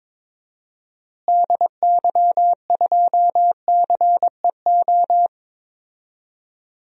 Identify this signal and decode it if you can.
Morse code, heard as a keyed tone, about 22 wpm, 700 Hz. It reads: DY2CEO
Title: DXCC-100 Call Signs - 6 Characters - Repeated 3x - 22wpm